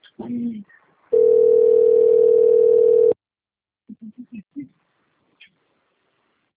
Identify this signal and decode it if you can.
Ringback tone (North America), 440 + 480 Hz.